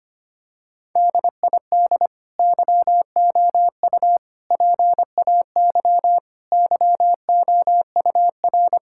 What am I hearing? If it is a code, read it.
Morse code, heard as a keyed tone, about 25 wpm, 700 Hz. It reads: DID YOU PAY YOUR
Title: Sentences from Top 1000 Words - Review Part 5 - 25wpm